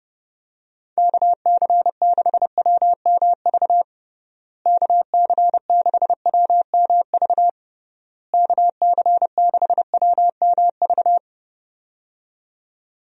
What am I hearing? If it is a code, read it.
Morse code, heard as a keyed tone, about 32 wpm, 700 Hz. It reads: KC6WMV KC6WMV KC6WMV
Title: USA Call Signs - 6 Characters - Repeated 3x - 30wpm